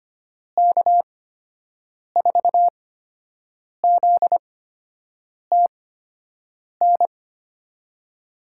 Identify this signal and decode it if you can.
Morse code, heard as a keyed tone, about 25 wpm, spaced as for 7 wpm, 700 Hz. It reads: K4ZTN
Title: USA Call Signs - 6 Characters - 8x Character Spacing - 25wpm